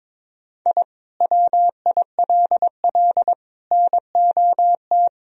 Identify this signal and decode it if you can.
Morse code, heard as a keyed tone, about 22 wpm, 700 Hz. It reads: I WILL NOT
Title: Sentences from Top 1000 Words - Review Part 1 - 22wpm